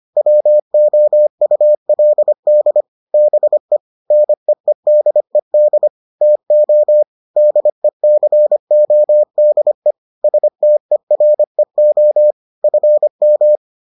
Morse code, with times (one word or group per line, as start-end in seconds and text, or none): 0.2-2.8 WOULD
3.1-3.8 BE
4.1-5.9 NEEDED
6.2-7.0 TO
7.4-9.9 DECODE
10.2-12.3 STEREO
12.6-13.6 FM